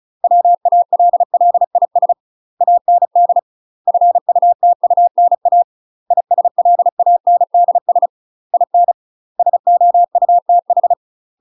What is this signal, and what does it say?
Morse code, heard as a keyed tone, about 36 wpm, 700 Hz. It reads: WALLIS AND FUTUNA ISLANDS IN SOUTH